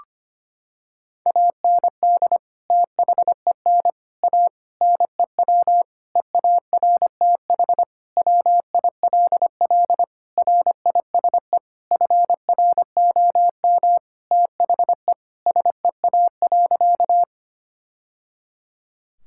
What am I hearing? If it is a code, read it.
Morse code, heard as a keyed tone, about 25 wpm, 700 Hz. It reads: AND THEN A NEW EARTH WILL RISE FROM THE SEA.
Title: Sentences from Top 1000 Words - Review Part 6 - 25wpm